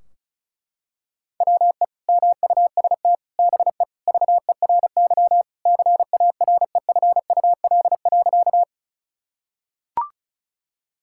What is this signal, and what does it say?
Morse code, heard as a keyed tone, about 36 wpm, 700 Hz. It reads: WE MUST BE VERY CAREFUL.
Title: Sentences from Top 1600 Words 35wpm